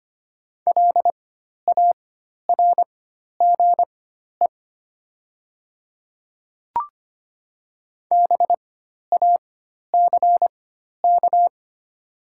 Morse code, heard as a keyed tone, about 25 wpm, spaced as for 12 wpm, 690 Hz. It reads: LARGE BACK
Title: 4x Character Spacing - Top 300 Words - 25wpm